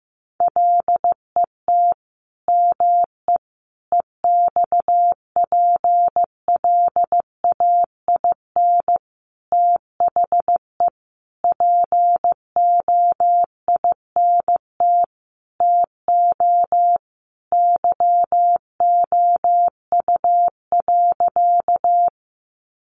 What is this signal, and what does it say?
Morse code, heard as a keyed tone, about 15 wpm, 700 Hz. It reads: LET ME EXPLAIN THE POINT TO YOU.
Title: Sentences from Top 1100 Words 15wpm